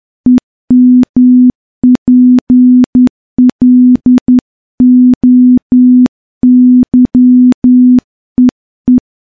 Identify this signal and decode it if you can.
Morse code, heard as a keyed tone, about 10 wpm, 250 Hz. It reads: EMPLOYEE
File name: morse-training-10WPM